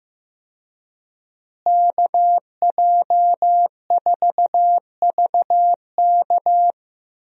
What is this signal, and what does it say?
Morse code, heard as a keyed tone, about 15 wpm, 700 Hz. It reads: KJ4VK
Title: USA Call Signs - 5 Characters - Repeated 3x - 15wpm